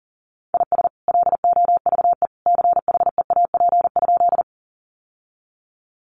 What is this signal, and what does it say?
Morse code, heard as a keyed tone, about 40 wpm, 700 Hz. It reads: IS LOVE CHEAP?
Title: Sentences from Top 1200 Words 40wpm